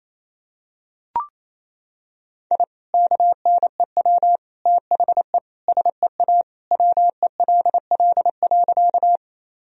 Morse code, heard as a keyed tone, about 28 wpm, 700 Hz. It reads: I KNEW THE SEA WELL.
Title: Sentences from Top 1000 Words - Review Part 1 - 28wpm